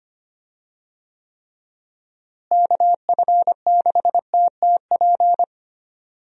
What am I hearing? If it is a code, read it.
Morse code, heard as a keyed tone, about 25 wpm, 700 Hz. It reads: KF6TTP
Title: USA Call Signs - 6 Characters - Repeated 3x - 25wpm